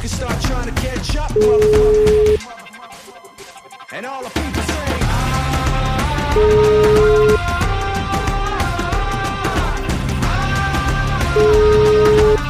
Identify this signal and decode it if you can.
Ringback tone (Europe, 425 Hz), with noise.